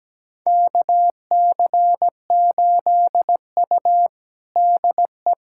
Morse code, heard as a keyed tone, about 17 wpm, 700 Hz. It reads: KC8U DE
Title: USA Callsign DE Callsign - 4 Characters - 17wpm